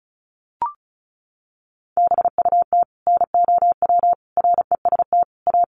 Morse code, heard as a keyed tone, about 34 wpm, 700 Hz. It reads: BUT NOW REST A
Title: Sentences from Top 1100 Words 35wpm